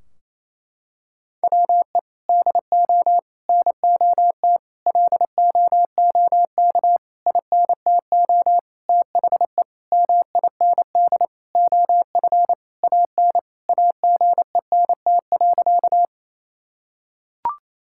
Morse code, heard as a keyed tone, about 28 wpm, 700 Hz. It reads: WE DO NOT LOOK INTO THE MIND OF AN AGENT.
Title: Sentences from Top 1600 Words 28wpm